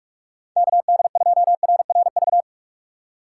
Morse code, heard as a keyed tone, about 46 wpm, 700 Hz. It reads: KD2RRV
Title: USA Call Signs - 6 Characters - 2x Character Spacing - 45wpm